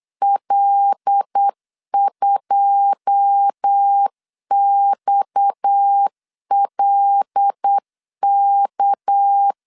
Morse code, heard as a keyed tone, about 8 wpm, 800 Hz. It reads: L2XLK
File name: LEZIONI / 10 SIGNS / 10C DOT 8wpm 3649